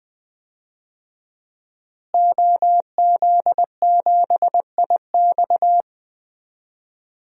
Morse code, heard as a keyed tone, about 20 wpm, 700 Hz. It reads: OZ7IX
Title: DXCC-100 Call Signs - 5 Characters - Repeated 3x - 20wpm